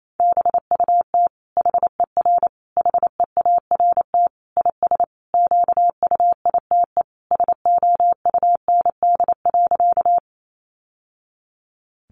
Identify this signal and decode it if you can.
Morse code, heard as a keyed tone, about 28 wpm, 700 Hz. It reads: BUT HER HEART IS QUITE SOUND.
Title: Sentences from Top 1200 Words 28wpm